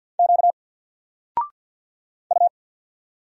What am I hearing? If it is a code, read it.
Morse code, heard as a keyed tone, about 50 wpm, 700 Hz. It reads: Q U
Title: Single Character (Letter:Number:Punctuation:Prosign) 50wpm